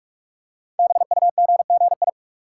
Morse code, heard as a keyed tone, about 44 wpm, 690 Hz. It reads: BUGGI
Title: Code Groups - 5 Letter:Numbers - 45wpm